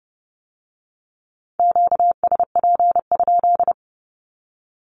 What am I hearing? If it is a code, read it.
Morse code, heard as a keyed tone, about 31 wpm, 700 Hz. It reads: QSP?
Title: Q-Codes with Spoken Meaning - 30wpm